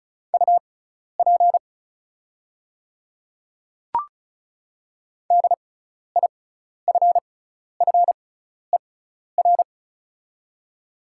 Morse code, heard as a keyed tone, about 34 wpm, spaced as for 12 wpm, 690 Hz. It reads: UP DIFFER
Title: 6x Character Spacing - Top 300 Words - 35wpm